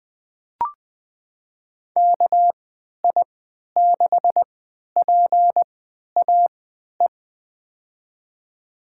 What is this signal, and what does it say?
Morse code, heard as a keyed tone, about 21 wpm, spaced as for 12 wpm, 700 Hz. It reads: KI6PAE
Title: USA Call Signs - 6 Characters - 3x Character Spacing - 20wpm